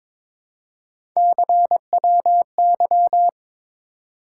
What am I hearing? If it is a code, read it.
Morse code, heard as a keyed tone, about 22 wpm, 700 Hz. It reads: CWY
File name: practice / 22wpm / Three Letter-Number Combinations - 22wpm